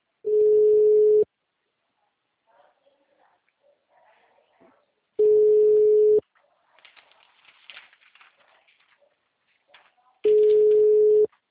Ringback tone (Europe, 425 Hz).